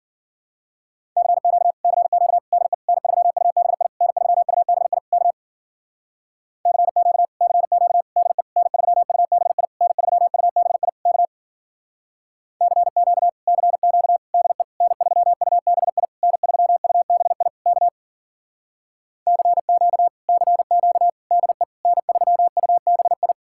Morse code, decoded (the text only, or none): CQ CQ DE N3UBI N3UBI K CQ CQ DE N3UBI N3UBI K CQ CQ DE N3UBI N3UBI K CQ CQ DE N3UBI